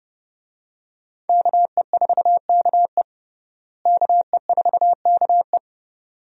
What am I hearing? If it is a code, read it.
Morse code, heard as a keyed tone, about 31 wpm, 700 Hz. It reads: KE4KE KE4KE
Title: USA Call Signs - 5 Characters - Repeated 2x - 30wpm